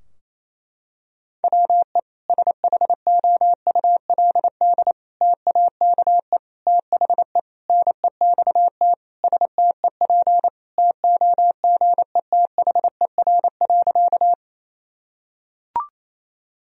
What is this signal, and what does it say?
Morse code, heard as a keyed tone, about 28 wpm, 700 Hz. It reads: WE SHOULD TAKE THE NEXT STEP TOGETHER.
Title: Sentences from Top 1000 Words - Review Part 2 - 28wpm